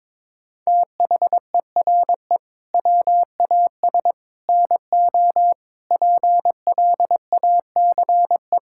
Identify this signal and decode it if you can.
Morse code, heard as a keyed tone, about 21 wpm, 700 Hz. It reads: THERE WAS NO PLACE
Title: Sentences from Top 1600 Words 22wpm